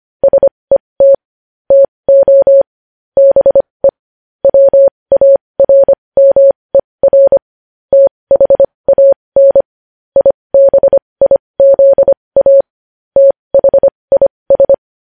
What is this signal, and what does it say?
Morse code, heard as a keyed tone, about 25 wpm, spaced as for 20 wpm, 550 Hz. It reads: SET TO BE WARMER THAN IBIZA THIS
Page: news-headlines-in-morse-code-at-20-wpm